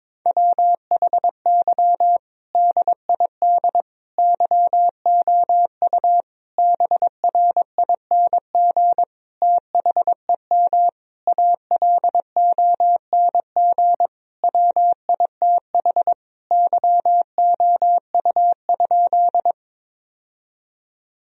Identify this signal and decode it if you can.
Morse code, heard as a keyed tone, about 22 wpm, 700 Hz. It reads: WHY DID YOU BRING THEM ALONG WITH YOU?
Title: Sentences from Top 1000 Words - Review Part 3 - 22wpm